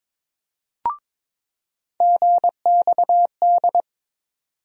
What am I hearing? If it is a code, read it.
Morse code, heard as a keyed tone, about 22 wpm, 700 Hz. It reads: GXD
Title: USA Call Sign Postfixes - 3 Letters 22wpm